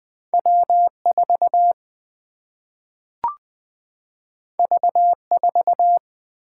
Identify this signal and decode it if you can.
Morse code, heard as a keyed tone, about 20 wpm, 700 Hz. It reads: W 4 V 4
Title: Single-Letter Single-Number 20wpm